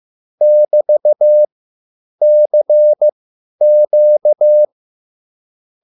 Morse code, heard as a keyed tone, about 15 wpm, spaced as for 9 wpm, 600 Hz. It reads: <BT>CQ